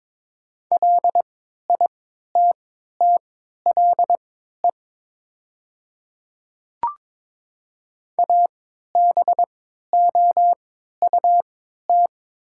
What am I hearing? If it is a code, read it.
Morse code, heard as a keyed tone, about 22 wpm, spaced as for 12 wpm, 700 Hz. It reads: LITTLE ABOUT
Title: 3x Character Spacing - Top 300 Words - 22wpm